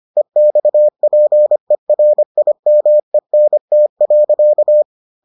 Morse code, heard as a keyed tone, about 25 wpm, 600 Hz. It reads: EXPERIMENT.